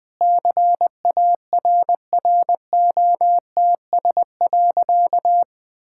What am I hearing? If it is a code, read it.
Morse code, heard as a keyed tone, about 20 wpm, 700 Hz. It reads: CARROTS.
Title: Easy Sentences with Hints - 20wpm